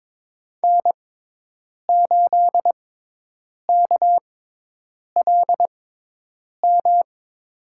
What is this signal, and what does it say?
Morse code, heard as a keyed tone, about 22 wpm, spaced as for 8 wpm, 700 Hz. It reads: N8KLM
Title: USA Call Signs - 6x Character Spacing - 22wpm